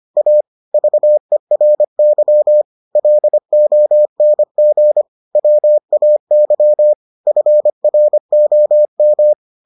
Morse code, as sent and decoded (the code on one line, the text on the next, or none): .- / ...- . .-. -.-- / .-.. --- -. --. / .-- .- -.-- / ..-. .-. --- --
A VERY LONG WAY FROM